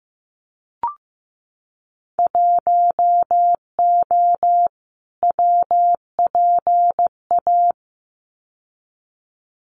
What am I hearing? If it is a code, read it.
Morse code, heard as a keyed tone, about 15 wpm, 700 Hz. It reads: 1O WPA
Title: Winter Field Day Exchanges 15wpm